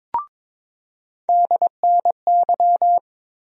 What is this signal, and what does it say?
Morse code, heard as a keyed tone, about 22 wpm, 700 Hz. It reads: DNY
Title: USA Call Sign Postfixes - 3 Letters 22wpm